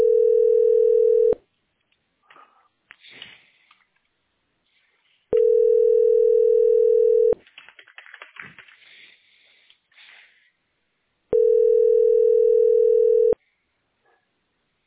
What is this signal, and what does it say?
Ringback tone (North America), 440 + 480 Hz.